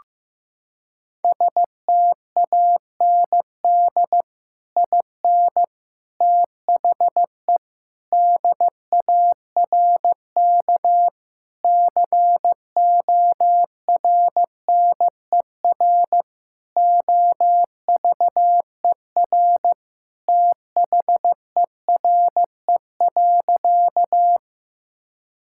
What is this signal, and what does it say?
Morse code, heard as a keyed tone, about 15 wpm, 700 Hz. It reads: STAND IN THE DARK CORNER OVER THERE.
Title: Sentences from Top 1600 Words 15wpm